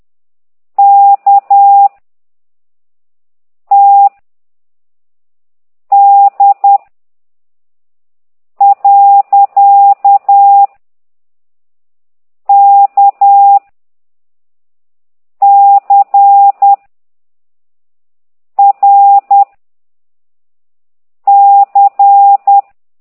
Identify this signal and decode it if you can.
Morse code, heard as a keyed tone, about 10 wpm, 800 Hz. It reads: K T D . K C R C